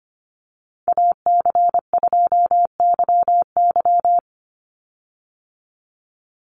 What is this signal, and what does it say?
Morse code, heard as a keyed tone, about 25 wpm, 700 Hz. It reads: AC2YY
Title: USA Call Signs 25wpm